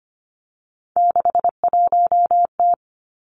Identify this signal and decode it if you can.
Morse code, heard as a keyed tone, about 25 wpm, 700 Hz. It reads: 61T
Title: Three Challenging Letter-Number Combinations 25wpm